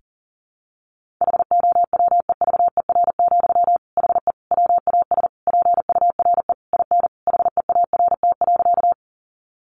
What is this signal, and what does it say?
Morse code, heard as a keyed tone, about 40 wpm, 700 Hz. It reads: HOWEVER, HE WAS PURE IN HEART.
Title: Sentences from Top 1500 Words 40wpm